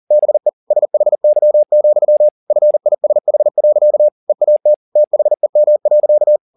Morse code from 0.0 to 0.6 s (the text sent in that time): BE